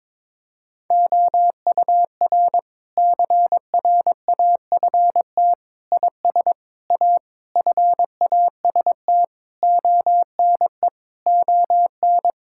Morse code, heard as a keyed tone, about 22 wpm, 700 Hz. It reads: OUR CRAFT IS A FAST ONE ON
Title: Sentences from Top 1000 Words - Review Part 6 - 22wpm